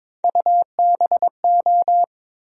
Morse code, heard as a keyed tone, about 22 wpm, 700 Hz. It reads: UBO
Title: USA Call Sign Postfixes - 3 Letters 22wpm